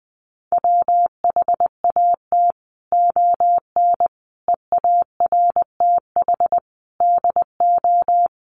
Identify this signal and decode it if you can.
Morse code, heard as a keyed tone, about 20 wpm, 700 Hz. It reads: WHAT ON EARTH DO
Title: Sentences from Top 200 Words 20wpm